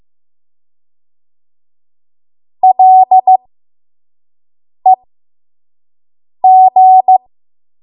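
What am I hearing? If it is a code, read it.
Morse code, heard as a keyed tone, about 15 wpm, spaced as for 5 wpm, 750 Hz. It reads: LEG
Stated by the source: 5 WPM Morse code practice.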